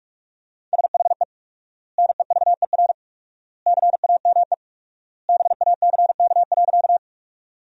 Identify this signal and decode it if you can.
Morse code, heard as a keyed tone, about 44 wpm, 700 Hz. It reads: SHE NEVER CAME BACK.
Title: 4x Word-Spacing - Sentences from Top 300 Words - 45wpm